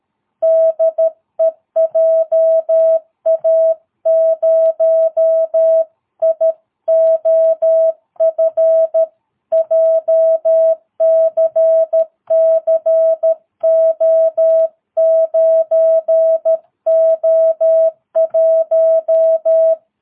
Morse code, heard as a keyed tone, about 13 wpm, 660 Hz. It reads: DE JA0IOF JCCO9O1
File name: jh8cbh / 2022cw